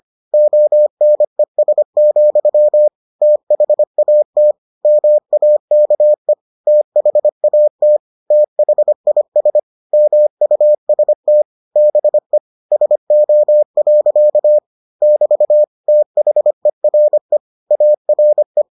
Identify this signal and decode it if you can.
Morse code, heard as a keyed tone, about 25 wpm, 600 Hz. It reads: ONES, THAT MAKE THAT THIS MUST BE SO. <BT> THERE ARE